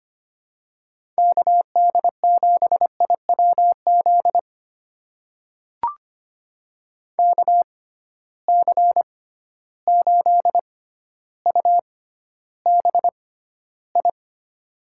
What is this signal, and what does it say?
Morse code, heard as a keyed tone, about 25 wpm, spaced as for 13 wpm, 700 Hz. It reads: KD7IWZ KC8UBI
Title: USA Call Signs - 6 Characters - 6x Character Spacing - 25wpm